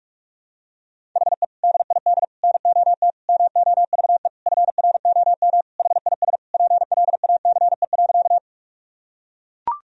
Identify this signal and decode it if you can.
Morse code, heard as a keyed tone, about 42 wpm, 700 Hz. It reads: HE DID NOT MOVE FROM HIS PLACE.
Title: Sentences from Top 1000 Words - Review Part 1 - 45wpm